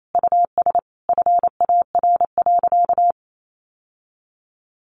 Morse code, heard as a keyed tone, about 28 wpm, 700 Hz. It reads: US FAR.
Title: Sentences from Top 1500 Words 28wpm